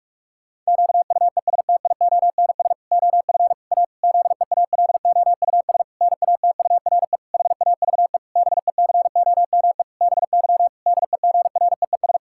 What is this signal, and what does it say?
Morse code, heard as a keyed tone, about 46 wpm, 700 Hz. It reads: QUESTIONS OF A ZEALOUS NATURE HAVE BECOME BY DEGREES
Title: Pangram Challenge v2 - 45wpm